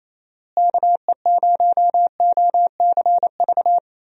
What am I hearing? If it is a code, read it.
Morse code, heard as a keyed tone, about 28 wpm, 700 Hz. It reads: KE0OCV
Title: USA Call Signs - 6 Characters - Repeated 2x - 28wpm